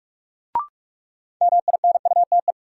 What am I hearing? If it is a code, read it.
Morse code, heard as a keyed tone, about 44 wpm, 700 Hz. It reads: MINUTE
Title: Sentences from Top 500 Words - New Word Drill - 45wpm